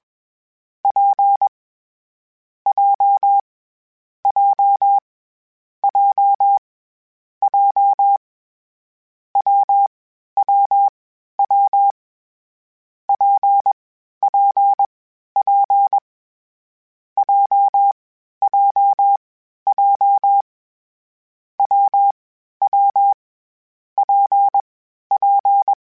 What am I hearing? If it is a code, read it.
Morse code, heard as a keyed tone, about 21 wpm, 800 Hz. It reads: P J J J J W W W P P P J J J W W P P
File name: Difficult Combos / Difficult Combos-mp3s / WPJ WPJ1 @20